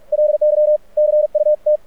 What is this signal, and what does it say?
Morse code, heard as a keyed tone, about 48 wpm, 600 Hz. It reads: LY CUT
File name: archived / CRSnet